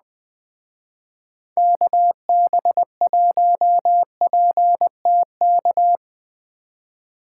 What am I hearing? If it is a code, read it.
Morse code, heard as a keyed tone, about 21 wpm, 700 Hz. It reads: KB1PTK